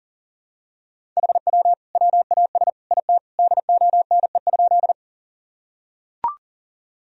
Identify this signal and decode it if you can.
Morse code, heard as a keyed tone, about 37 wpm, 700 Hz. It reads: HW WAS IT DONE?
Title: Sentences from Top 500 Words and Abbreviations 40wpm